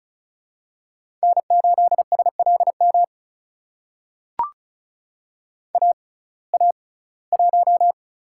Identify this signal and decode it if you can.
Morse code, heard as a keyed tone, about 36 wpm, spaced as for 21 wpm, 700 Hz. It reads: N8SLM AA1
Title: USA Call Signs - 5 Characters - 6x Character Spacing - 35wpm